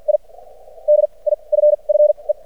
Morse code, heard as a keyed tone, about 56 wpm, spaced as for 33 wpm, 600 Hz. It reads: I DIFFE
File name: archived / CRSnet